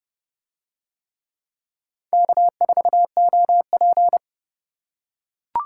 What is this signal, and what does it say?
Morse code, heard as a keyed tone, about 27 wpm, 700 Hz. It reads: K4OP